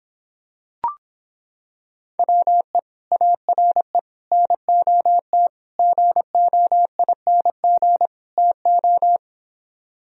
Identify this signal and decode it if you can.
Morse code, heard as a keyed tone, about 26 wpm, 700 Hz. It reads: WE ARE NOT GOING TO